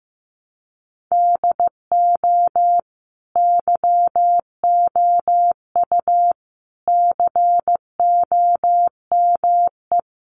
Morse code, heard as a keyed tone, about 15 wpm, 700 Hz. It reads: DO YOU COME
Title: Sentences from Top 200 Words 15wpm